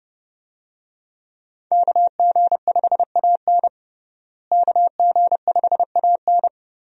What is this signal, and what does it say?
Morse code, heard as a keyed tone, about 29 wpm, 700 Hz. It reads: KG5AN KG5AN
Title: USA Call Signs - 5 Characters - Repeated 2x - 30wpm